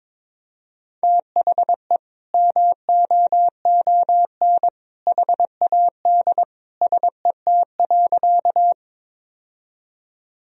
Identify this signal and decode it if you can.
Morse code, heard as a keyed tone, about 22 wpm, 700 Hz. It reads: THE MOON HAD SET.
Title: Sentences from Top 1000 Words - Review Part 3 - 22wpm